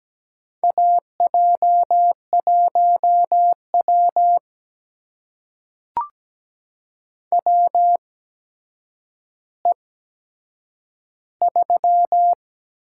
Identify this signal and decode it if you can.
Morse code, heard as a keyed tone, about 17 wpm, spaced as for 8 wpm, 700 Hz. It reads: AJ1W WE3
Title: USA Call Signs - 4 Characters - 8x Character Spacing - 17wpm